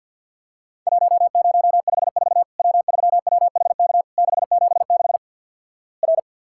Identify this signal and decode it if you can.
Morse code, heard as a keyed tone, about 53 wpm, 700 Hz. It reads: 1054 W3WHK 676 R
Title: Worked All Europe DX Contest - QTC Exchange - 50wpm